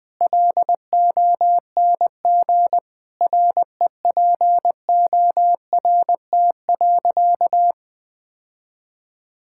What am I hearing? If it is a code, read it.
Morse code, heard as a keyed tone, about 20 wpm, 700 Hz. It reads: LONG REPORT.
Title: Sentences from Top 1000 Words - Review Part 6 - 20wpm